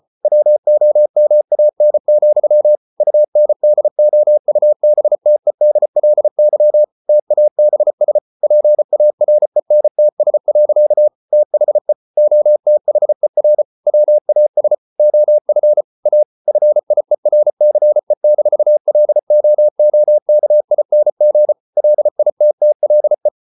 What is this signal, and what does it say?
Morse code, heard as a keyed tone, about 34 wpm, 600 Hz. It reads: WOMAN, UNDOUBTEDLY TABS PARENTS. THE OTHER WAS OF A FIERCE-LOOKING LITTLE